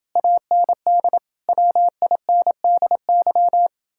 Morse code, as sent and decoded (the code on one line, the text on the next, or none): .- -. -.. / .-- .. -. -.. -.--
AND WINDY